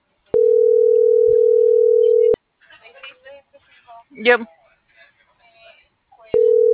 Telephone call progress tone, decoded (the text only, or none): ringback tone